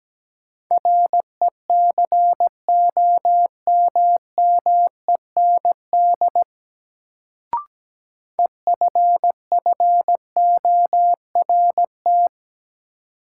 Morse code, heard as a keyed tone, about 17 wpm, 700 Hz. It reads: RECOMMEND EFFORT
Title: Sentences from Top 1200 Words - New Word Drill Alt - 17wpm